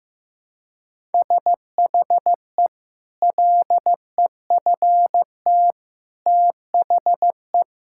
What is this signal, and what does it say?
Morse code, heard as a keyed tone, about 16 wpm, 700 Hz. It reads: SHE LEFT THE
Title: Sentences from Top 1000 Words - Review Part 5 - 15wpm